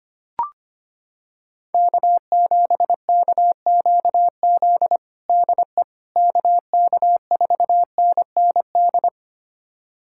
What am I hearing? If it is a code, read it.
Morse code, heard as a keyed tone, about 25 wpm, 700 Hz. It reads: K7KQZ DE KK4NND
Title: USA Callsign DE Callsign 25wpm